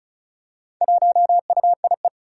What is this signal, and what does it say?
Morse code, heard as a keyed tone, about 36 wpm, 700 Hz. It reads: 1UIE
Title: Four Challenging Letter-Number Combinations 35wpm